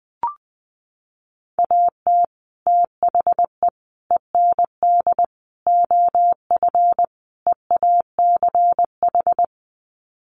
Morse code, heard as a keyed tone, about 20 wpm, 700 Hz. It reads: AT THE END OF EACH
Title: Sets of 5 Words Encore 20wpm